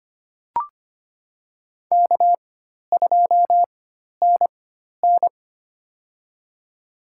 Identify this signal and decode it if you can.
Morse code, heard as a keyed tone, about 25 wpm, spaced as for 12 wpm, 700 Hz. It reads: K2NN
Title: USA Call Signs - 4x Character Spacing - 25wpm